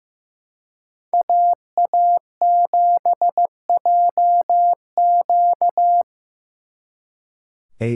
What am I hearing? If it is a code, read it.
Morse code, heard as a keyed tone, about 18 wpm, 700 Hz. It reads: AA7JQ
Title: Speed Racing - USA 2x2 Call Signs 15wpm